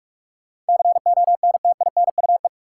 Morse code, heard as a keyed tone, about 46 wpm, 700 Hz. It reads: CONTINUE
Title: Sentences from Top 1100 Words - New Word Drill Alt - 45wpm